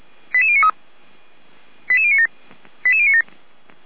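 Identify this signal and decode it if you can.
Five-tone selective call (ZVEI), heard as quick runs of five tones, tones of about 70 ms. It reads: 88093 88097 88097
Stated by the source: ZVEI five-tone selective call received by SDR, code 88057.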